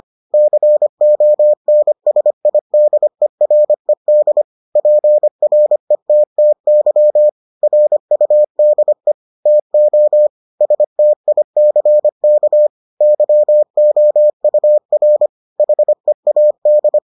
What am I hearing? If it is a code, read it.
Morse code, heard as a keyed tone, about 25 wpm, 600 Hz. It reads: CONSIDERED PRETTY RUDE TO STICK YOUR HEAD